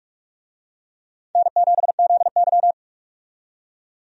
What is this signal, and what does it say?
Morse code, heard as a keyed tone, about 46 wpm, 700 Hz. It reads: N7ZY
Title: USA Call Signs - 4 Characters - 45wpm